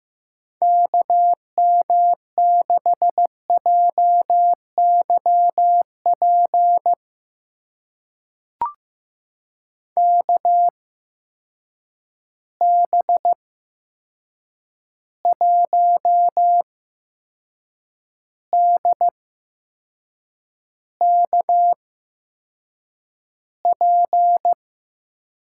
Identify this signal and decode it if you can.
Morse code, heard as a keyed tone, about 15 wpm, spaced as for 6 wpm, 700 Hz. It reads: KM6JYP KB1DKP
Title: USA Call Signs - 6 Characters - 8x Character Spacing - 15wpm